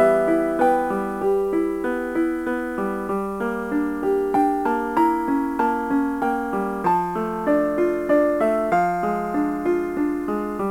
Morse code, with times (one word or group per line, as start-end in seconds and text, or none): none